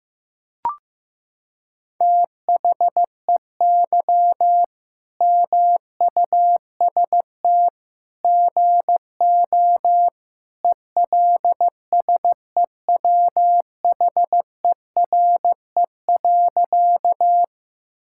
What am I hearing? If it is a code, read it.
Morse code, heard as a keyed tone, about 16 wpm, 700 Hz. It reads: THEY MUST GO ELSEWHERE.
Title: Sentences from Top 1600 Words 15wpm